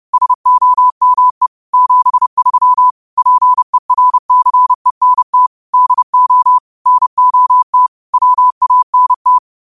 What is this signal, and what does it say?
Morse code, heard as a keyed tone, about 30 wpm, 1000 Hz. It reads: SOME 73 PERCENT DO NOT WANT
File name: qotdfiles / 30wpm